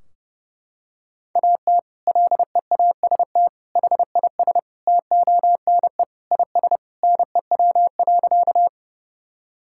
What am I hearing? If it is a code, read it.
Morse code, heard as a keyed tone, about 30 wpm, 700 Hz. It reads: AT LEAST HIS TONE IS NEW.
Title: Sentences from Top 1600 Words 30wpm